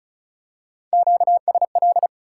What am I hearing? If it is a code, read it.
Morse code, heard as a keyed tone, about 36 wpm, 700 Hz. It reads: QSL
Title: Top 180 QSO Elements 35wpm